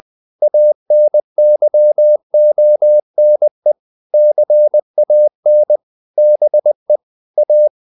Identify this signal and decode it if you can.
Morse code, heard as a keyed tone, about 20 wpm, 600 Hz. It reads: ANYONE CAN BE A
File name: the old man and the sea / the old man and the sea part 1 20wpm